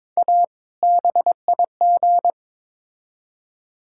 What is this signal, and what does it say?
Morse code, heard as a keyed tone, about 22 wpm, 700 Hz. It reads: A BIG
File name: Build A Phrase - 2-Word Phrases - 22wpm